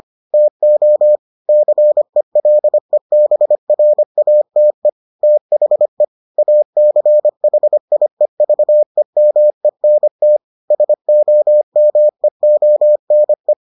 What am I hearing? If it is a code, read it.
Morse code, heard as a keyed tone, about 25 wpm, 600 Hz. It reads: TO CELEBRATE THE ACHIEVEMENT SOMEONE